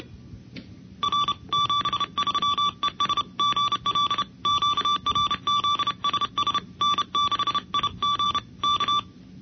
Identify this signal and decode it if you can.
Morse code, heard as a keyed tone, about 31 wpm, 1160 Hz. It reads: R 73ES GL QRZSS N6IG K
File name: hf / NCCC sweepstakes tape2